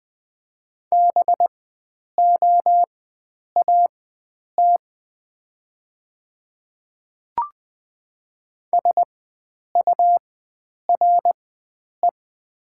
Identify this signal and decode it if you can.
Morse code, heard as a keyed tone, about 20 wpm, spaced as for 9 wpm, 700 Hz. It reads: BOAT SURE
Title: 4x Character Spacing - Top 500 Words - 20wpm